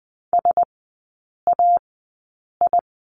Morse code, heard as a keyed tone, about 20 wpm, spaced as for 8 wpm, 700 Hz. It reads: SAI
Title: 035 Lesson 7e - Letter s - Three Characters 20wpm